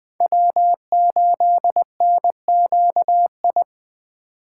Morse code, heard as a keyed tone, about 23 wpm, 700 Hz. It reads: W8NQI
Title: Speed Racing - USA 1x3 Call Signs 20wpm